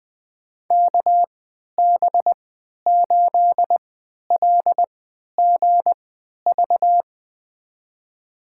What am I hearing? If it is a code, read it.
Morse code, heard as a keyed tone, about 20 wpm, spaced as for 11 wpm, 700 Hz. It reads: KB8LGV
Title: USA Call Signs - 3x Character Spacing - 20wpm